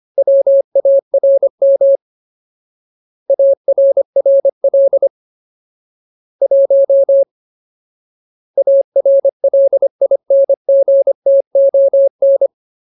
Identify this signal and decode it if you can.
Morse code, heard as a keyed tone, about 25 wpm, 550 Hz. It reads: WARM ARRL 1 ARLINGTON
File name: Ultimate Word List / 25WPM Ultimate